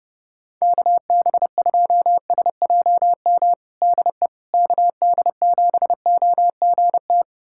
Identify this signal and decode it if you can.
Morse code, heard as a keyed tone, about 30 wpm, 700 Hz. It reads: KB2SJM DE KD7OGT
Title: USA Callsign DE Callsign - 6 Characters - 30wpm